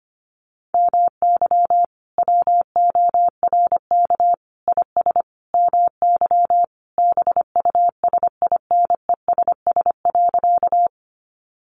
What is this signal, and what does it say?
Morse code, heard as a keyed tone, about 25 wpm, 700 Hz. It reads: MY WORK IS MY BUSINESS.
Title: Sentences from Top 600 Words 25wpm